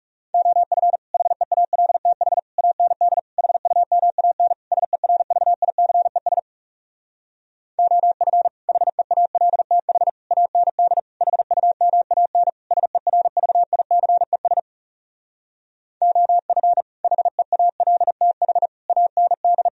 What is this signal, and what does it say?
Morse code, heard as a keyed tone, about 40 wpm, 700 Hz. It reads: OF HEALTH AND HUMAN SERVICES OF HEALTH AND HUMAN SERVICES OF HEALTH AND